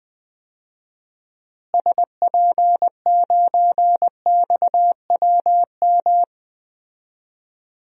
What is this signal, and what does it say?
Morse code, heard as a keyed tone, about 21 wpm, 700 Hz. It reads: SP9XWM
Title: DXCC-100 Call Signs - 6 Characters - Repeated 3x - 20wpm